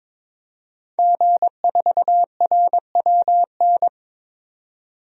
Morse code, heard as a keyed tone, about 21 wpm, 700 Hz. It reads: G4RWN
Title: DXCC-100 Call Signs - 5 Characters - 22wpm